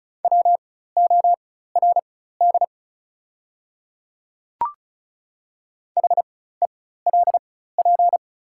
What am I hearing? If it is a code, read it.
Morse code, heard as a keyed tone, about 34 wpm, spaced as for 16 wpm, 700 Hz. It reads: WORD HELP
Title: 4x Character Spacing - Top 200 Words - 35wpm